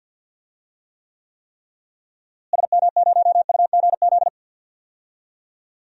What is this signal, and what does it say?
Morse code, heard as a keyed tone, about 50 wpm, 700 Hz. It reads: SM0UGZ
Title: DXCC-100 Call Signs - 6 Characters - Repeated 3x - 50wpm